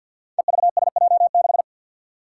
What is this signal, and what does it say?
Morse code, heard as a keyed tone, about 47 wpm, 700 Hz. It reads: EVSJ6
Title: Five Challenging Letter-Number Combinations 50wpm